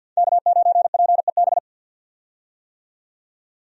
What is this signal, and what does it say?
Morse code, heard as a keyed tone, about 48 wpm, 700 Hz. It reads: K9PEB
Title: USA Call Signs - 5 Characters - 50wpm